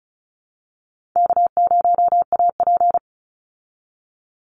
Morse code, heard as a keyed tone, about 34 wpm, 700 Hz. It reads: K0AP
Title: Call Signs Repeated 2x - 35wpm